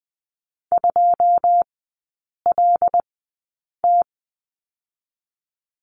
Morse code, heard as a keyed tone, about 20 wpm, spaced as for 8 wpm, 700 Hz. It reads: 2LT
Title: 127 Lesson 21e - Number 6 - Three Characters 20wpm